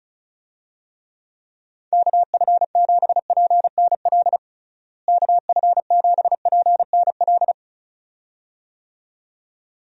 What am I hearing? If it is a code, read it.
Morse code, heard as a keyed tone, about 36 wpm, 700 Hz. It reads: KF7PNL KF7PNL
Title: USA Call Signs - 6 Characters - Repeated 2x - 35wpm